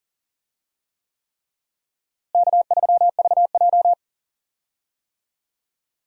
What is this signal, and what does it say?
Morse code, heard as a keyed tone, about 40 wpm, 700 Hz. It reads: K3VJ
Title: USA Call Signs - 4 Characters - Repeated 3x - 40wpm